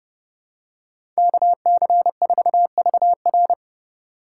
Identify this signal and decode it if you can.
Morse code, heard as a keyed tone, about 30 wpm, 700 Hz. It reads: KC4VR
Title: USA Call Signs - 5 Characters - 6x Character Spacing - 30wpm